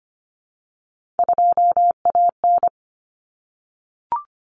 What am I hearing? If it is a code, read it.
Morse code, heard as a keyed tone, about 25 wpm, 700 Hz. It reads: 2AN
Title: Common Cut Numbers - 3-Digit 25wpm